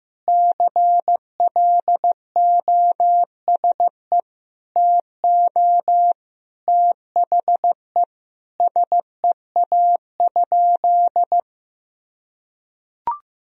Morse code, heard as a keyed tone, about 15 wpm, 700 Hz. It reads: CLOSE TO THE SEA?
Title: Sentences from Top 300 Words and Abbreviations 15wpm